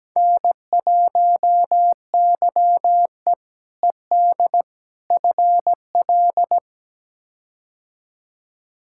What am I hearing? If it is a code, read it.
Morse code, heard as a keyed tone, about 17 wpm, 700 Hz. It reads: N1YE ED FL
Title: K1USN SST QSO Practice - 17wpm